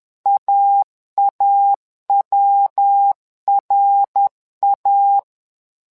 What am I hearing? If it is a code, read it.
Morse code, heard as a keyed tone, about 10 wpm, 800 Hz. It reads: AAWRA
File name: LEZIONI / 3 MWRO / 3 MWRO 10wpm 2148